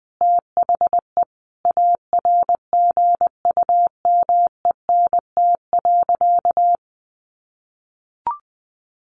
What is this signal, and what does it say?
Morse code, heard as a keyed tone, about 20 wpm, 700 Hz. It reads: THE ARGUMENT.
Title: Sentences from Top 1400 Words 20wpm